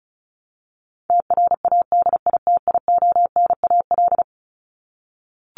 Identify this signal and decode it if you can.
Morse code, heard as a keyed tone, about 34 wpm, 700 Hz. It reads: TRADITIONAL
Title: Top 751-1000 Words 35wpm